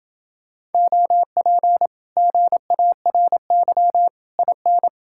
Morse code, heard as a keyed tone, about 29 wpm, 700 Hz. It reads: OP GARY IN